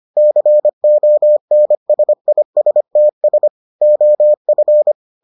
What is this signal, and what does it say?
Morse code, heard as a keyed tone, about 25 wpm, 600 Hz. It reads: CONSISTS OF